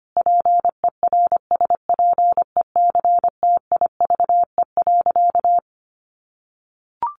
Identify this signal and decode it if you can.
Morse code, heard as a keyed tone, about 25 wpm, 700 Hz. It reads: PERSPECTIVE.
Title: Sentences from Top 1200 Words 25wpm